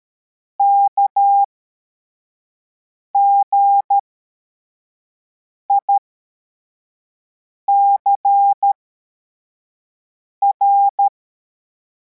Morse code, heard as a keyed tone, about 13 wpm, spaced as for 4 wpm, 800 Hz. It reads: KGICR
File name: formation / cw / Text33